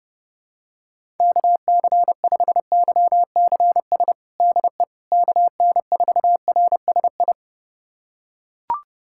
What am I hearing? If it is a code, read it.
Morse code, heard as a keyed tone, about 30 wpm, 700 Hz. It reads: KC5YCS DE KN4RSI
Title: USA Callsign DE Callsign - 6 Characters - 30wpm